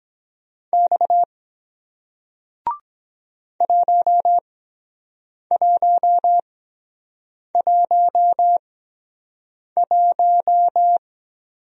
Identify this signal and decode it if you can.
Morse code, heard as a keyed tone, about 23 wpm, spaced as for 7 wpm, 700 Hz. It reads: X 1111